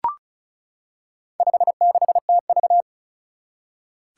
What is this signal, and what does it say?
Morse code, heard as a keyed tone, about 36 wpm, 700 Hz. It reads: 56TV
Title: Four Challenging Letter-Number Combinations 35wpm